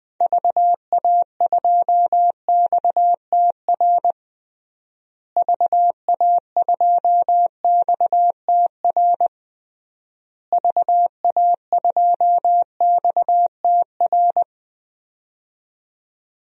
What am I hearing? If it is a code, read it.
Morse code, heard as a keyed tone, about 20 wpm, 700 Hz. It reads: VA2XTR VA2XTR VA2XTR
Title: DXCC-100 Call Signs - 6 Characters - Repeated 3x - 20wpm